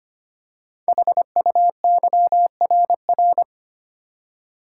Morse code, heard as a keyed tone, about 25 wpm, 700 Hz. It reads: HUYRR
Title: Code Groups - 5 Letters - 25wpm